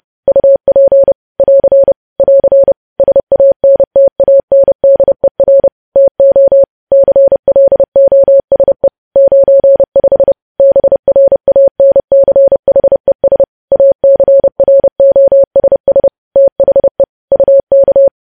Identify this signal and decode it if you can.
Morse code, heard as a keyed tone, about 30 wpm, 550 Hz. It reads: UP <AR> <AR> SANTANDER TO CLOSE 95 BRANCHES ACROSS THE UK
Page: news headlines in morse code at 30 wpm